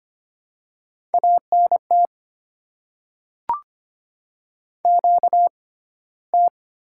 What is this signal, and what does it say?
Morse code, heard as a keyed tone, about 25 wpm, spaced as for 15 wpm, 700 Hz. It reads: ANT QT
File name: Top 20 QSO Elements - 6x Character Spacing - 25wpm